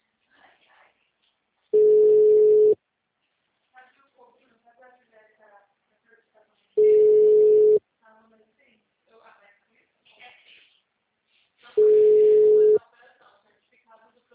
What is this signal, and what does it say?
Ringback tone (Europe, 425 Hz), with speech.
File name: out- -2004- - - .30